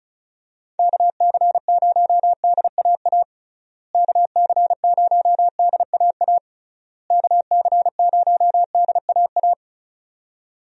Morse code, heard as a keyed tone, about 34 wpm, 700 Hz. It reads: KC0DAA KC0DAA KC0DAA
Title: USA Call Signs - 6 Characters - Repeated 3x - 35wpm